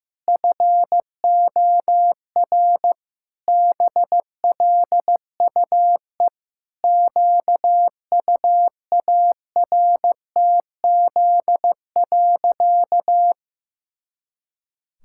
Morse code, heard as a keyed tone, about 15 wpm, 700 Hz. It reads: FOR BLUE QUARTZ.
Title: Pangram Challenge v2 - 15wpm